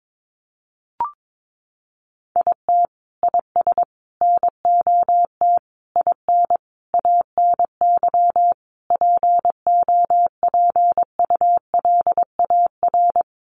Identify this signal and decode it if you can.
Morse code, heard as a keyed tone, about 22 wpm, 700 Hz. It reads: IT IS NOT IN ANY POPULAR
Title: Sentences from Top 1200 Words 22wpm